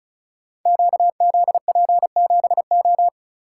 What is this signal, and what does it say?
Morse code, heard as a keyed tone, about 36 wpm, 700 Hz. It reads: QZP7O
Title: Code Groups - 5 Letter:Numbers - 35wpm